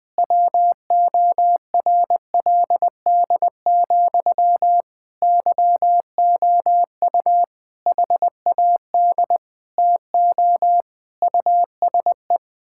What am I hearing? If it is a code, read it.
Morse code, heard as a keyed tone, about 20 wpm, 700 Hz. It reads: WORLD, YOU HAD TO USE
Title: Sentences from Top 1000 Words - Review Part 6 - 20wpm